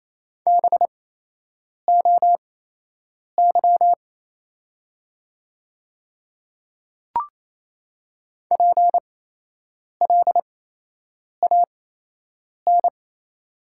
Morse code, heard as a keyed tone, about 28 wpm, spaced as for 8 wpm, 700 Hz. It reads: BOY PLAN
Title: 8x Character Spacing - Top 500 Words - 28wpm